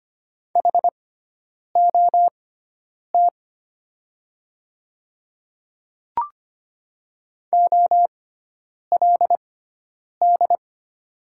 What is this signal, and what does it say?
Morse code, heard as a keyed tone, about 25 wpm, spaced as for 9 wpm, 700 Hz. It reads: HOT OLD
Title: 6x Character Spacing - Top 300 Words - 25wpm